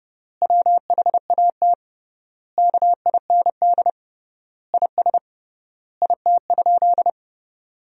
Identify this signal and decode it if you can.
Morse code, heard as a keyed tone, about 32 wpm, 700 Hz. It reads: WHAT KIND IS IT?
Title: 3x Word-Spacing - Sentences from Top 200 Words - 30wpm